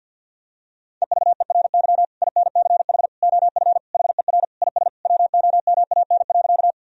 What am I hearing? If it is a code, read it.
Morse code, heard as a keyed tone, about 50 wpm, 700 Hz. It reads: EVERY INCH OF HER IS WOMAN.
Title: Sentences from Top 1000 Words - Review Part 3 - 50wpm